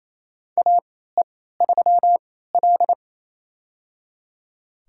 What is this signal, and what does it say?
Morse code, heard as a keyed tone, about 28 wpm, spaced as for 16 wpm, 700 Hz. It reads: AE3L
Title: USA Call Signs - 4 Characters - 3x Character Spacing - 28wpm